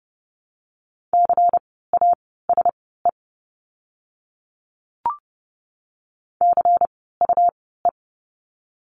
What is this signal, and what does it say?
Morse code, heard as a keyed tone, about 30 wpm, spaced as for 17 wpm, 700 Hz. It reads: CASE CUE
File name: Lesson 17f - Letter u - Words 30wpm